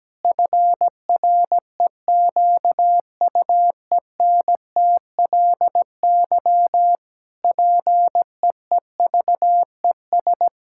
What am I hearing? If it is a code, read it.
Morse code, heard as a keyed tone, about 17 wpm, 700 Hz. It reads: FREQUENTLY PEEVES
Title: Pangram Challenge v2 - 17wpm